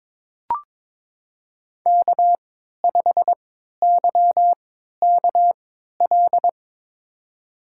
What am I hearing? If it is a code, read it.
Morse code, heard as a keyed tone, about 22 wpm, spaced as for 13 wpm, 700 Hz. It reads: K5YKL
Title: USA Call Signs - 5 Characters - 3x Character Spacing - 22wpm